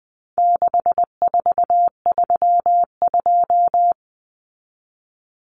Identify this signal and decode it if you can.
Morse code, heard as a keyed tone, about 20 wpm, 700 Hz. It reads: 6432
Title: Common Cut Numbers - 4-Digit 20wpm